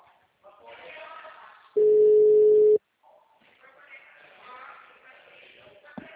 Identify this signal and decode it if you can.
Ringback tone (Europe, 425 Hz), with speech.
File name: out- -2016- - - .7